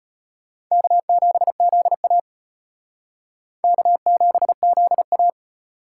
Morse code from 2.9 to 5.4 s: K7ZA

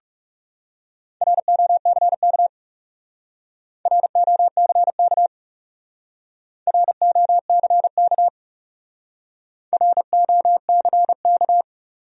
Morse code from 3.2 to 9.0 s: ROCK ROCK

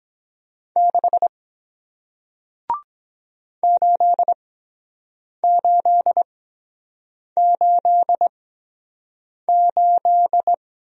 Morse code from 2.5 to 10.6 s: 8888

